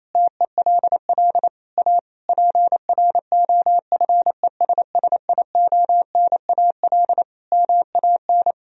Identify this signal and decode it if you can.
Morse code, heard as a keyed tone, about 28 wpm, 700 Hz. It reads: TELL A PROFESSIONAL MAN
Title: Sentences from Top 1000 Words - Review Part 4 - 28wpm